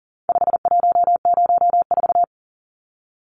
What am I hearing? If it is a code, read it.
Morse code, heard as a keyed tone, about 40 wpm, 700 Hz. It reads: 5104
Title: Numbers - 4-Digit 40wpm